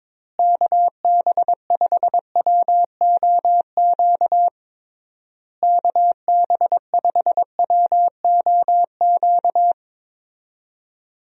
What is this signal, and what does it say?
Morse code, heard as a keyed tone, about 22 wpm, 700 Hz. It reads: KB5WOQ KB5WOQ
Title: USA Call Signs - 6 Characters - Repeated 3x - 22wpm